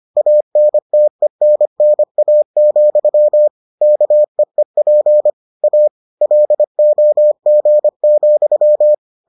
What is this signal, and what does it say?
Morse code, heard as a keyed tone, about 25 wpm, 600 Hz. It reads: ANTENNA, KEEP A LOG,